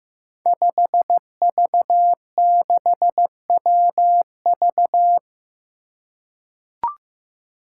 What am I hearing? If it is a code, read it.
Morse code, heard as a keyed tone, about 15 wpm, 700 Hz. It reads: 5V6WV
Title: Five Challenging Letter-Number Combinations 15wpm